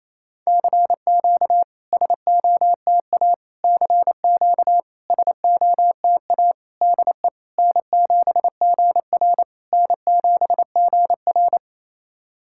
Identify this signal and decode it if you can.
Morse code, heard as a keyed tone, about 28 wpm, 700 Hz. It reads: CQ SOTA CQ SOTA DE N7GR N7GR
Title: SOTA QSO Practice 28wpm